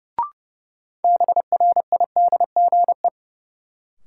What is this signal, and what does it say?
Morse code, heard as a keyed tone, about 30 wpm, 700 Hz. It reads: BRIDGE
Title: Sentences from Top 1200 Words - New Word Drill Alt - 30wpm